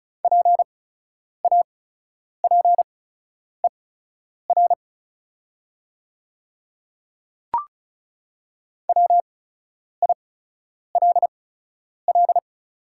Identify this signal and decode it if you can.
Morse code, heard as a keyed tone, about 36 wpm, spaced as for 10 wpm, 690 Hz. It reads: PAPER WILL
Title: 8x Character Spacing - Top 300 Words - 35wpm